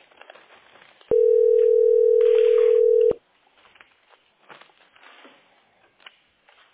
Ringback tone (North America), 440 + 480 Hz.
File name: 0a0f1d6f-1cdf-4f1a-82c6